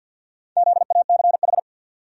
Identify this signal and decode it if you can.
Morse code, heard as a keyed tone, about 48 wpm, 700 Hz. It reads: ZACH
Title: USA Common Names - 50wpm